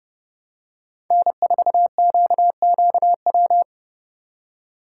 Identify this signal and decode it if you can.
Morse code, heard as a keyed tone, about 30 wpm, 700 Hz. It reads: N4QQW
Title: USA Call Signs - 5 Characters - 3x Character Spacing - 30wpm